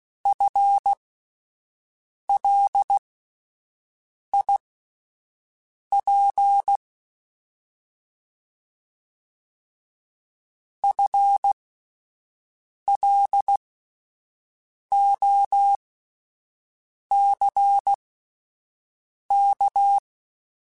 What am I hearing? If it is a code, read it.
Morse code, heard as a keyed tone, about 16 wpm, spaced as for 5 wpm, 800 Hz. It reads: FLIP FLOCK
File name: Spaced / FL Runs sp mp3s / FL Runs sp @10